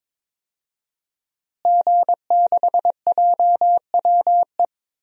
Morse code, heard as a keyed tone, about 22 wpm, 700 Hz. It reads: G6JWE
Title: DXCC-100 Call Signs - 5 Characters - Repeated 3x - 22wpm